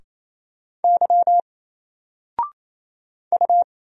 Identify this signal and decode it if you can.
Morse code, heard as a keyed tone, about 27 wpm, 700 Hz. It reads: Y U
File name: Single-Letter-Number-28wpm